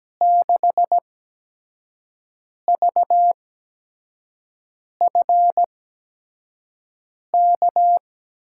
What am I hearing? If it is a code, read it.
Morse code, heard as a keyed tone, about 17 wpm, spaced as for 5 wpm, 700 Hz. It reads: 6VFK
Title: USA Call Signs - 6 Characters - 8x Character Spacing - 17wpm